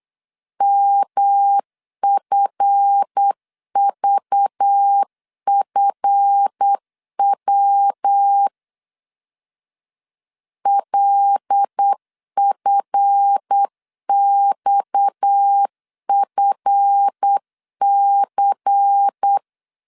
Morse code, heard as a keyed tone, about 8 wpm, 800 Hz. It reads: MFVFW LFXFC
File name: LEZIONI / 7 GF9Z / 7 GF 8wpm 3108